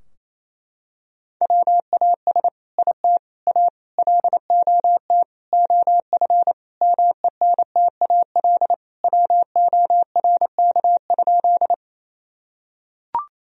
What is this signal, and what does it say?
Morse code, heard as a keyed tone, about 28 wpm, 700 Hz. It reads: WAS IT A LOT OF MENTAL WORK?
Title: Sentences from Top 1000 Words - Review Part 4 - 28wpm